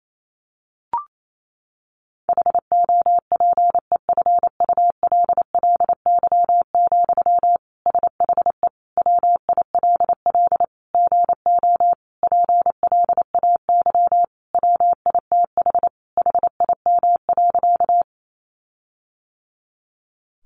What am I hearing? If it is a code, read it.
Morse code, heard as a keyed tone, about 28 wpm, 700 Hz. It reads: HOPEFULLY, SHE WILL GO PLAY WITH HIM.
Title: Sentences from Top 1500 Words 28wpm